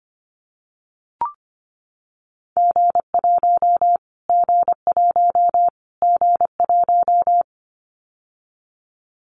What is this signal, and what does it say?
Morse code, heard as a keyed tone, about 25 wpm, 700 Hz. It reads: G1 G1 G1
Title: Callsign Prefixes for Top 25 DXCC - Repeated 3x - 25wpm